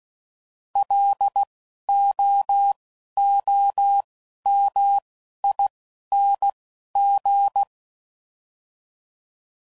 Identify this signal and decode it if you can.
Morse code, heard as a keyed tone, about 16 wpm, spaced as for 11 wpm, 800 Hz. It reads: LOOMING
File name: Prefixes / By First Letter / Ls